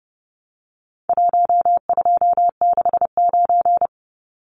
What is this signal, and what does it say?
Morse code, heard as a keyed tone, about 30 wpm, 700 Hz. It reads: 1269
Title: Numbers - Contest Style - 4-Digit 30wpm